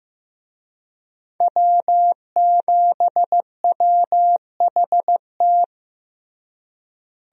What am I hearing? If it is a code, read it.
Morse code, heard as a keyed tone, about 15 wpm, 700 Hz. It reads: W7WHT
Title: USA Call Signs - 5 Characters - Repeated 3x - 15wpm